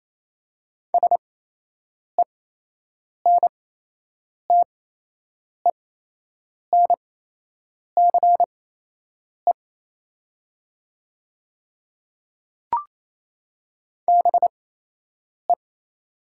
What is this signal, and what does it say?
Morse code, heard as a keyed tone, about 28 wpm, spaced as for 8 wpm, 700 Hz. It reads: SENTENCE BE
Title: 8x Character Spacing - Top 200 Words - 28wpm